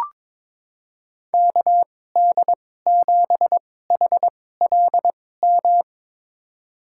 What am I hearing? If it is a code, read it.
Morse code, heard as a keyed tone, about 22 wpm, spaced as for 16 wpm, 700 Hz. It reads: KD7HLM
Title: USA Call Signs - 6 Characters - 2x Character Spacing - 22wpm